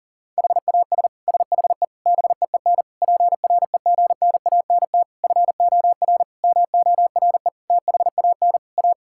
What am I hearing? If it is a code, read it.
Morse code, heard as a keyed tone, about 40 wpm, 700 Hz. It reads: HAS SHE BEEN PREGNANT FOR MORE THAN A
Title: Sentences from Top 1000 Words - Review Part 6 - 40wpm